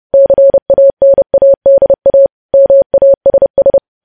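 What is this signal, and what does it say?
Morse code, heard as a keyed tone, about 30 wpm, 550 Hz. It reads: CANADA MASS
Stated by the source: Morse code news headlines sent at 30, preceded by vvv vvv.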